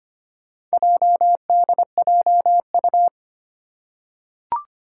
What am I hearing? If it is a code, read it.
Morse code, heard as a keyed tone, about 25 wpm, 700 Hz. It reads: JDJU
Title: Four Challenging Letter-Number Combinations 25wpm